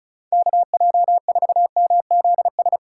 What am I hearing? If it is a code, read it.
Morse code, heard as a keyed tone, about 33 wpm, 700 Hz. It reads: KJ4MZS
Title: Speed Racing - USA Call Signs 35wpm